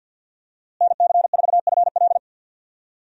Morse code, heard as a keyed tone, about 51 wpm, 700 Hz. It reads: NC4FL
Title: USA Call Signs - 5 Characters - Repeated 2x - 50wpm